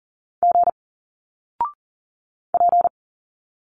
Morse code, heard as a keyed tone, about 40 wpm, 700 Hz. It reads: G P
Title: Single Letters 40wpm